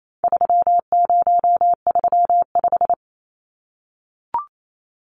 Morse code, heard as a keyed tone, about 28 wpm, 700 Hz. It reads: 3035
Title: Numbers - 4-Digit 28wpm